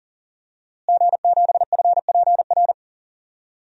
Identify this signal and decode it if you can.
Morse code, heard as a keyed tone, about 38 wpm, 690 Hz. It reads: G7FPR
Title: DXCC-100 Call Signs - 5 Characters - Repeated 2x - 40wpm